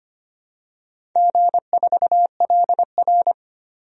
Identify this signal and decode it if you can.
Morse code, heard as a keyed tone, about 25 wpm, 700 Hz. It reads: G4LR
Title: DXCC-100 Call Signs - 4 Characters - 25wpm